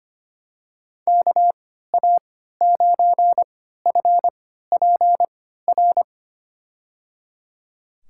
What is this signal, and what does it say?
Morse code, heard as a keyed tone, about 25 wpm, spaced as for 14 wpm, 700 Hz. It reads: KA9FPR
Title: USA Call Signs - 6 Characters - 3x Character Spacing - 25wpm